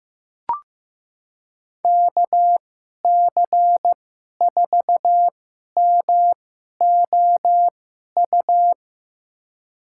Morse code, heard as a keyed tone, about 15 wpm, spaced as for 11 wpm, 700 Hz. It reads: KC4MOU
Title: USA Call Signs - 6 Characters - 2x Character Spacing - 15wpm